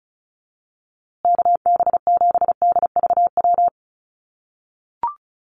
Morse code, heard as a keyed tone, about 34 wpm, 700 Hz. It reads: KB7DVW
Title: USA Call Signs Repeated 3x 35wpm